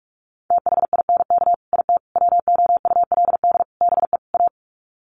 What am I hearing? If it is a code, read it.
Morse code, heard as a keyed tone, about 44 wpm, 690 Hz. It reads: THINK IT WOULD BE A
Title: Sets of 5 Words Encore2 45wpm